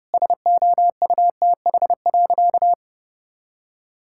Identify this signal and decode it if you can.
Morse code, heard as a keyed tone, about 30 wpm, 700 Hz. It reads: SOUTH.
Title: Sentences from Top 1000 Words - Review Part 6 - 30wpm